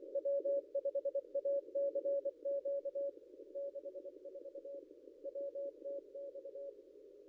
Morse code, heard as a keyed tone, about 24 wpm, 560 Hz. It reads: W5ACQ 64 WTX
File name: contests / ss-cw-k5tr-op-n5rz -0700